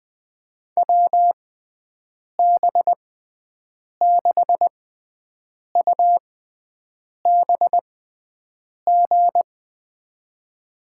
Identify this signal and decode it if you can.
Morse code, heard as a keyed tone, about 21 wpm, spaced as for 7 wpm, 700 Hz. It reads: WB6UBG
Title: USA Call Signs - 6x Character Spacing - 20wpm